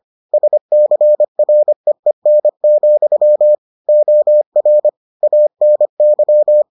Morse code, heard as a keyed tone, about 25 wpm, 600 Hz. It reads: SCREEN, OR ANY